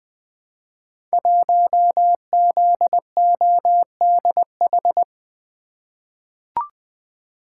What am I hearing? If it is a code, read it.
Morse code, heard as a keyed tone, about 20 wpm, 700 Hz. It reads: 1ZODH
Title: Code Groups - 5 Letter:Numbers - 20wpm